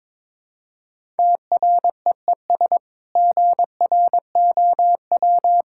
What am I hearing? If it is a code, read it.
Morse code, heard as a keyed tone, about 22 wpm, 700 Hz. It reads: TREES GROW
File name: practice / 22wpm / Easy Sentences with Hints - 22wpm